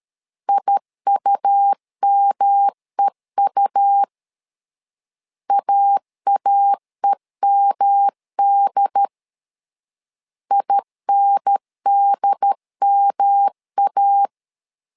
Morse code, heard as a keyed tone, about 13 wpm, 800 Hz. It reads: IUMEU AAEMD INDMA
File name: LEZIONI / 3 MWRO / 3 M 12wpm 1541